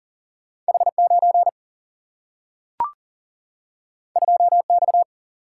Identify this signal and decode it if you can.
Morse code, heard as a keyed tone, about 40 wpm, 700 Hz. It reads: H9 2X
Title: Single-Letter Single-Number 40wpm